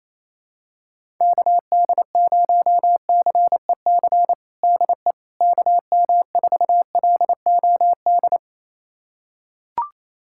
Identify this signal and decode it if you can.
Morse code, heard as a keyed tone, about 28 wpm, 700 Hz. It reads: KD0CEC DE KM4LOD
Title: USA Callsign DE Callsign - 6 Characters - 28wpm